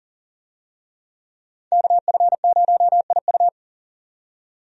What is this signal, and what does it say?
Morse code, heard as a keyed tone, about 40 wpm, 700 Hz. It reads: KF0IU
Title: USA Call Signs - 5 Characters - 40wpm